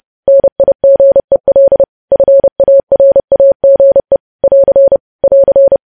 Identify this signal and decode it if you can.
Morse code, heard as a keyed tone, about 30 wpm, 550 Hz. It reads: NIGEL FARAGE <AR> <AR>
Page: news headlines in morse code at 30 wpm